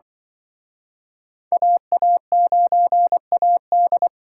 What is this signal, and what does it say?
Morse code, heard as a keyed tone, about 26 wpm, 700 Hz. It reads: AA9AD